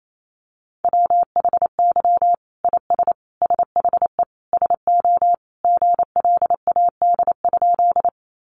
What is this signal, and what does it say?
Morse code, heard as a keyed tone, about 28 wpm, 700 Hz. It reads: WHY IS SHE SO GLAD?
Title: Sentences from Top 800 Words 28wpm